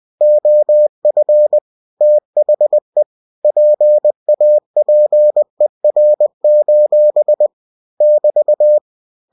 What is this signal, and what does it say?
Morse code, heard as a keyed tone, about 20 wpm, 600 Hz. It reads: OF THE PAPER: <BT>
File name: cw-books / winnie-the-pooh / winnie-20wpm-page-0092